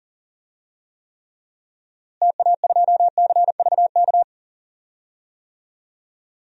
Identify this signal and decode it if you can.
Morse code, heard as a keyed tone, about 37 wpm, 700 Hz. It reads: TA2CVK
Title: DXCC-100 Call Signs - 6 Characters - Repeated 3x - 40wpm